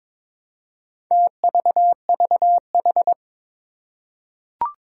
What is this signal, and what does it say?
Morse code, heard as a keyed tone, about 22 wpm, 700 Hz. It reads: TVVH
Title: Four Challenging Letter-Number Combinations 22wpm